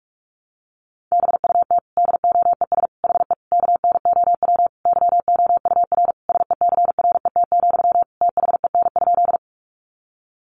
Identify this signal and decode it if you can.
Morse code, heard as a keyed tone, about 44 wpm, 700 Hz. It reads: BUT DOES HE KNOW YOUR SECRET, THEN?
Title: Sentences from Top 1200 Words 45wpm